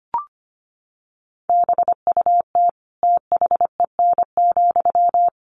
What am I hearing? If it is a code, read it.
Morse code, heard as a keyed tone, about 25 wpm, 700 Hz. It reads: BUT THEN,
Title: Sentences from Top 1400 Words 25wpm